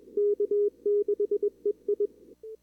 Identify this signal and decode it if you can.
Morse code, heard as a keyed tone, about 33 wpm, spaced as for 27 wpm, 410 Hz, with radio noise. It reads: K6EI
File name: CWSprint / Feb2022 / k5tr-feb-na-sprint-cw-k5ot -0000